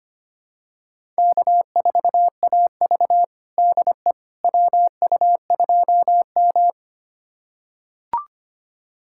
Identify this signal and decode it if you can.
Morse code, heard as a keyed tone, about 25 wpm, 700 Hz. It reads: K4AV DE WU2M
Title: USA Callsign DE Callsign - 4 Characters - 25wpm